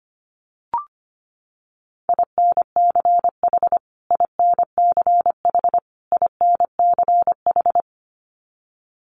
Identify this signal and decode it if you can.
Morse code, heard as a keyed tone, about 25 wpm, 700 Hz. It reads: INCH INCH INCH
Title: Top 500 Words Repeated 25wpm